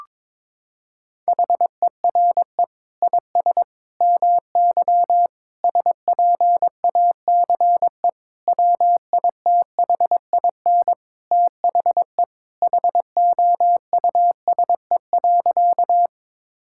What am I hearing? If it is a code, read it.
Morse code, heard as a keyed tone, about 22 wpm, 700 Hz. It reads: HERE IS MY SPACE WITHIN THE HOUSE.
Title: Sentences from Top 1000 Words - Review Part 5 - 22wpm